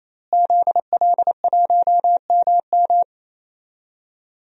Morse code, heard as a keyed tone, about 28 wpm, 700 Hz. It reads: ZL1MM
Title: DXCC-100 Call Signs - 5 Characters - Repeated 3x - 28wpm